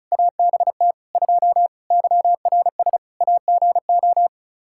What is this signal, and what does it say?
Morse code, heard as a keyed tone, about 36 wpm, 700 Hz. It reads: ABT 2 YRS AGO
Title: Practice QSO - Exchange 3 - More Info - 35wpm